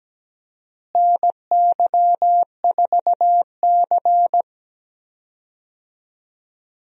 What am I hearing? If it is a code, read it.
Morse code, heard as a keyed tone, about 17 wpm, 700 Hz. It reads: NY4C
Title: USA Call Signs - 4 Characters - Repeated 3x - 17wpm